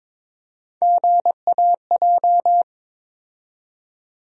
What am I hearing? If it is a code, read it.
Morse code, heard as a keyed tone, about 22 wpm, 700 Hz. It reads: GAJ
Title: USA Call Sign Postfixes - 3 Letters 22wpm